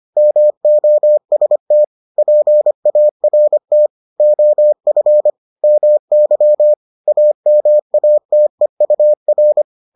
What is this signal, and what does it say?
Morse code, heard as a keyed tone, about 25 wpm, 600 Hz. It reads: MOST PART OF MY AMATEUR